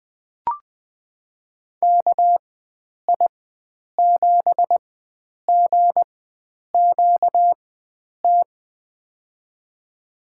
Morse code, heard as a keyed tone, about 20 wpm, spaced as for 9 wpm, 700 Hz. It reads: KI7GQT
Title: USA Call Signs - 6 Characters - 4x Character Spacing - 20wpm